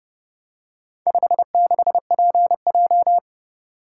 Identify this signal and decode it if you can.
Morse code, heard as a keyed tone, about 31 wpm, 700 Hz. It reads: 56PJ